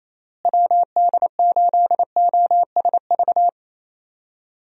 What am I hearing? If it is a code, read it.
Morse code, heard as a keyed tone, about 28 wpm, 700 Hz. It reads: WD8OSV
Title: USA Call Signs - 6 Characters - Repeated 3x - 28wpm